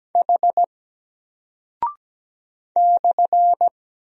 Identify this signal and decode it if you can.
Morse code, heard as a keyed tone, about 17 wpm, 700 Hz. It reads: H /
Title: Single Character (Letter:Number:Punctuation:Prosign) Alt 17wpm